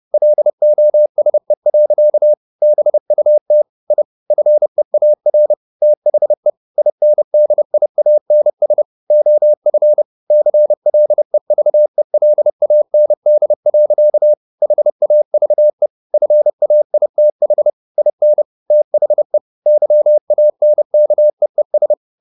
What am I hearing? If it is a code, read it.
Morse code, heard as a keyed tone, about 30 wpm, 600 Hz. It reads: LOSE. BUT I FEAR THE INDIANS OF CLEVELAND. HAVE FAITH IN THE YANKEES
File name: the old man and the sea / the old man and the sea 1 30wpm